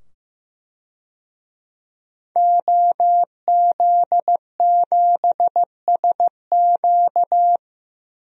Morse code, heard as a keyed tone, about 15 wpm, 700 Hz. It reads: OZ7SQ
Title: DXCC-100 Call Signs - 5 Characters - 15wpm